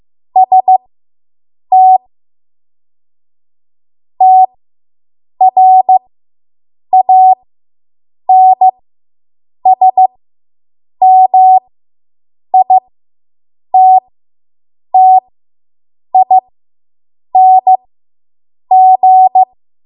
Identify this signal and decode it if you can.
Morse code, heard as a keyed tone, about 15 wpm, spaced as for 7 wpm, 750 Hz. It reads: ST TRANSMITTING